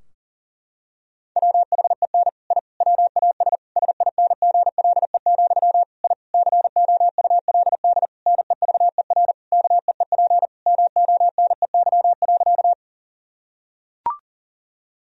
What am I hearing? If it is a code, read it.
Morse code, heard as a keyed tone, about 40 wpm, 700 Hz. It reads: WHEN I WAS SINGLE, I COULD NEVER KEEP MONEY.
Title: Sentences from Top 1000 Words - Review Part 4 - 40wpm